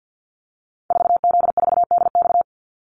Morse code, heard as a keyed tone, about 50 wpm, 700 Hz. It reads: 4Z4DX
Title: Call Signs Repeated 3x - 50wpm